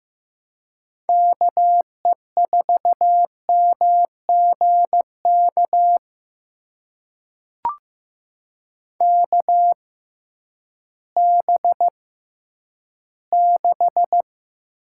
Morse code, heard as a keyed tone, about 15 wpm, spaced as for 10 wpm, 700 Hz. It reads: KE4MGK KB6
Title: USA Call Signs - 6 Characters - 6x Character Spacing - 15wpm